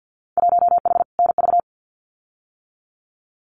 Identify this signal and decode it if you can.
Morse code, heard as a keyed tone, about 50 wpm, 700 Hz. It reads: 1H NV
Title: Winter Field Day Exchanges 50wpm